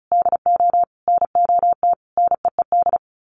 Morse code, heard as a keyed tone, about 34 wpm, 700 Hz. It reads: DO NOT NEED
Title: Sets of 4 Words 35wpm